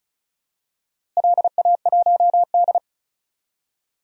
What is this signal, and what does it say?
Morse code, heard as a keyed tone, about 36 wpm, 700 Hz. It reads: LA1D
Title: DXCC-100 Call Signs - 4 Characters - Repeated 3x - 35wpm